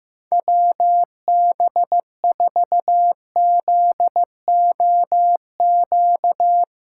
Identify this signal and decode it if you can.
Morse code, heard as a keyed tone, about 16 wpm, 700 Hz. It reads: WB4ZOQ
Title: USA Callsign DE Callsign 15wpm